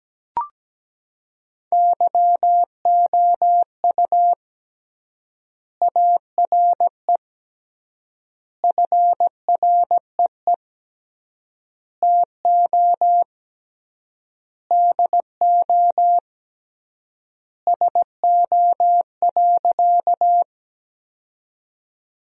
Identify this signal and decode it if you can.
Morse code, heard as a keyed tone, about 17 wpm, 700 Hz. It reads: YOU ARE FREE TO DO SO.
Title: 3x Word-Spacing - Sentences from Top 500 Words - 17wpm